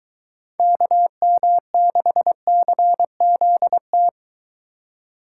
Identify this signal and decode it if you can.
Morse code, heard as a keyed tone, about 22 wpm, 700 Hz. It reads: KM6CZT